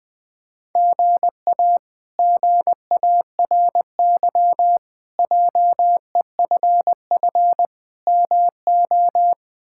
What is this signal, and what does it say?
Morse code, heard as a keyed tone, about 20 wpm, 700 Hz. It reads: GA GARY JEFF MO
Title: K1USN SST QSO Practice - 20wpm